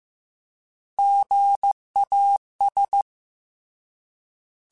Morse code, heard as a keyed tone, about 15 wpm, 800 Hz. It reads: GAS